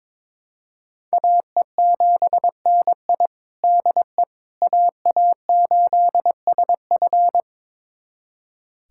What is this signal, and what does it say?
Morse code, heard as a keyed tone, about 22 wpm, 700 Hz. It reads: AE7NI DE AA8SF
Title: USA Callsign DE Callsign - 5 Characters - 22wpm